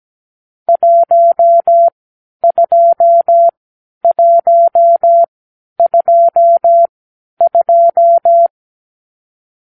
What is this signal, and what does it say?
Morse code, heard as a keyed tone, about 17 wpm, spaced as for 10 wpm, 680 Hz. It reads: 12122
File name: k7qo / Lesson 44 (1-3)